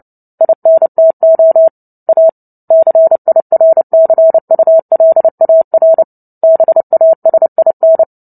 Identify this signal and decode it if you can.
Morse code, heard as a keyed tone, about 30 wpm, 650 Hz. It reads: INTO A CIRCULAR BASIN
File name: Book Two / 03-Chapter 3 - 29 WPM